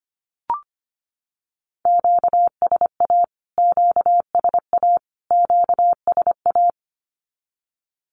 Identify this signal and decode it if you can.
Morse code, heard as a keyed tone, about 25 wpm, 700 Hz. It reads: QSA QSA QSA
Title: Q-Codes Repeated 3x 25wpm